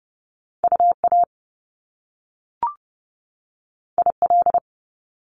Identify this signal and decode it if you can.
Morse code, heard as a keyed tone, about 30 wpm, 700 Hz. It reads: UA IL
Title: Top Two Letter Combinations in Words 30wpm